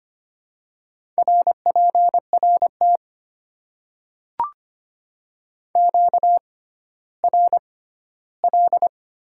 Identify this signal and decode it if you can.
Morse code, heard as a keyed tone, about 25 wpm, spaced as for 14 wpm, 700 Hz. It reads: RPRT QRL
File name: Top 90 QSO Elements - 6x Character Spacing - 25wpm